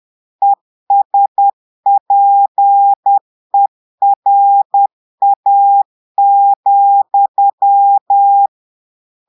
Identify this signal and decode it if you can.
Morse code, heard as a keyed tone, about 10 wpm, 800 Hz. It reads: ESPERA,
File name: Portugais - texte en clair 1 - vitesse - 10 mots minute